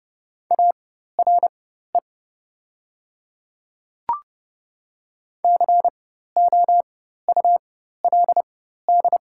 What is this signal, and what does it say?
Morse code, heard as a keyed tone, about 32 wpm, spaced as for 14 wpm, 700 Hz. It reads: ARE COULD
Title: 4x Character Spacing - Top 300 Words - 30wpm